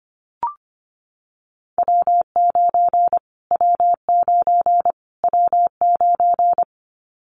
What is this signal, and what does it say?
Morse code, heard as a keyed tone, about 25 wpm, 700 Hz. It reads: W9 W9 W9
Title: Callsign Prefixes for Top 25 DXCC - Repeated 3x - 25wpm